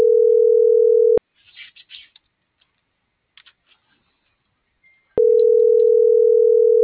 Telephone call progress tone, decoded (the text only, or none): ringback tone